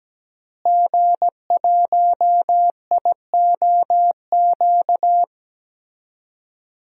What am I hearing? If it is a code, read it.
Morse code, heard as a keyed tone, about 17 wpm, 700 Hz. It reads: G1IOQ
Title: DXCC-100 Call Signs - 5 Characters - Repeated 3x - 17wpm